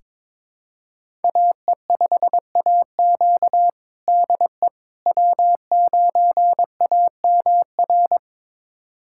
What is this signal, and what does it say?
Morse code, heard as a keyed tone, about 22 wpm, 700 Hz. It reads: AE5AQ DE W9AMR
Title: USA Callsign DE Callsign - 5 Characters - 22wpm